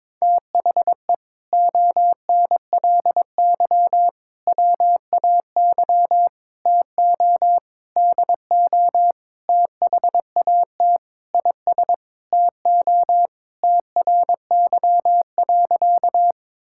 Morse code, heard as a keyed tone, about 22 wpm, 700 Hz. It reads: THE ONLY WAY TO DO THAT IS TO TRY.
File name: practice / 22wpm / Sentences with 8-Word Phrases - 22wpm